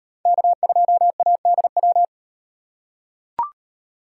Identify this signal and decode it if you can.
Morse code, heard as a keyed tone, about 38 wpm, 700 Hz. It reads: K2ADW